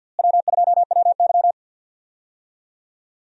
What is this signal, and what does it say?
Morse code, heard as a keyed tone, about 51 wpm, 700 Hz. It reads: W2WY
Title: USA Call Signs - 4 Characters - Repeated 3x - 50wpm